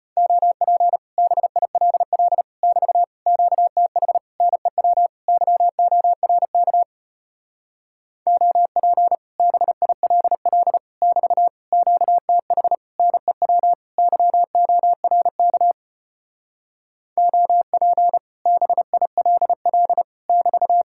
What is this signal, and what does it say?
Morse code, heard as a keyed tone, about 33 wpm, 700 Hz. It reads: OP BILL <BT> QTH NEW YORK OP BILL <BT> QTH NEW YORK OP BILL <BT>